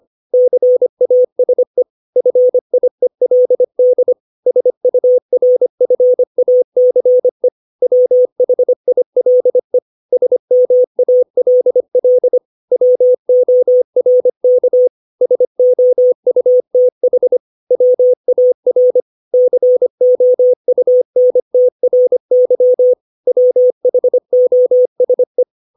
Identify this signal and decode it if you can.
Morse code, heard as a keyed tone, about 25 wpm, 500 Hz. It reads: CASE FIELD SURFACE WHILE SMALL WORK SOUTH WAR COUNTRY WHOSE
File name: 632 words / 632 Words 25WPM